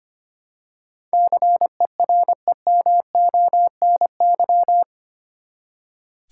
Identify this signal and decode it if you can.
Morse code, heard as a keyed tone, about 25 wpm, 700 Hz. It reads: CEREMONY
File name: Eight Letter Words - 25wpm